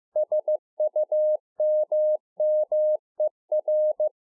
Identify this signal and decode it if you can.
Morse code, heard as a keyed tone, about 15 wpm, 610 Hz. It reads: SUMMER